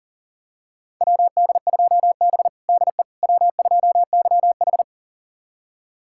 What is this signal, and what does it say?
Morse code, heard as a keyed tone, about 40 wpm, 690 Hz. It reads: WD2B DE W2YH
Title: USA Callsign DE Callsign - 4 Characters - 40wpm